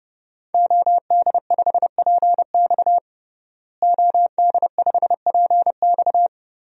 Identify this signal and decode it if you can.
Morse code, heard as a keyed tone, about 30 wpm, 700 Hz. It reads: OD5PX OD5PX
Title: DXCC-100 Call Signs - 5 Characters - Repeated 3x - 30wpm